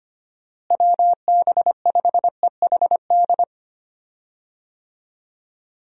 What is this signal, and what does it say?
Morse code, heard as a keyed tone, about 25 wpm, 700 Hz. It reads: WB5EHD
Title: USA Call Signs - 6 Characters - Repeated 2x - 25wpm